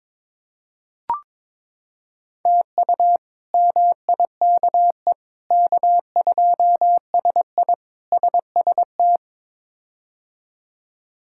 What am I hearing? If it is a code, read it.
Morse code, heard as a keyed tone, about 22 wpm, 700 Hz. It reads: TU MIKE K2SI SST
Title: K1USN SST QSO Practice - 22wpm